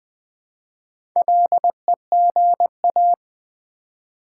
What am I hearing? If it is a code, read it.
Morse code, heard as a keyed tone, about 21 wpm, 700 Hz. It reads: LEGA
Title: Build A Word - 1 Letter At A Time - 3 to 5-Letter Words - 20wpm